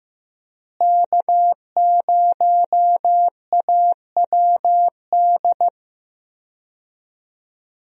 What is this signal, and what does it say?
Morse code, heard as a keyed tone, about 16 wpm, 700 Hz. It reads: K0AWD
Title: USA Call Signs - 5 Characters - Repeated 2x - 15wpm